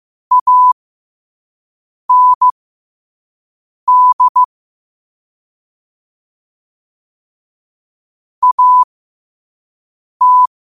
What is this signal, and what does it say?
Morse code, heard as a keyed tone, about 15 wpm, spaced as for 5 wpm, 1000 Hz. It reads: AND AT
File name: qotdfiles / 5wpm